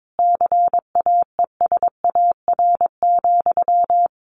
Morse code, heard as a keyed tone, about 22 wpm, 700 Hz. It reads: CAESAR,
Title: A Sound Of Thunder 22wpm